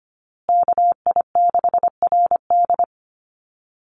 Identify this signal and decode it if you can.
Morse code, heard as a keyed tone, about 25 wpm, 700 Hz. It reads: KI6RD
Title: USA Call Signs Repeated 2x 25wpm